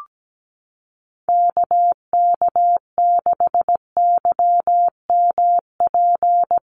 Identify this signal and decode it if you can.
Morse code, heard as a keyed tone, about 17 wpm, 700 Hz. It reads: KK6YMP
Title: USA Call Signs Repeated 3x 17wpm